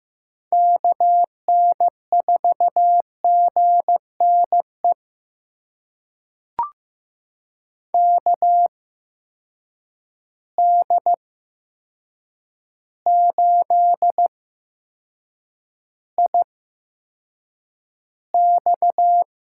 Morse code, heard as a keyed tone, about 15 wpm, spaced as for 7 wpm, 700 Hz. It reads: KN4GNE KD8IX
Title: USA Call Signs - 8x Character Spacing - 15wpm